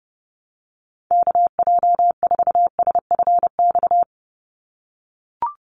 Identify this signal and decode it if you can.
Morse code, heard as a keyed tone, about 30 wpm, 700 Hz. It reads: KJ4SFX
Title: USA Call Signs 30wpm